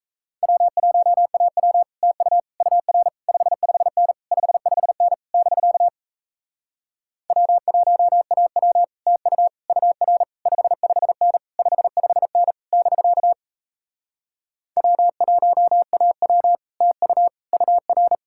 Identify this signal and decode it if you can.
Morse code, heard as a keyed tone, about 37 wpm, 700 Hz. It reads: W1AW TU UR 55N 55N <BK> W1AW TU UR 55N 55N <BK> W1AW TU UR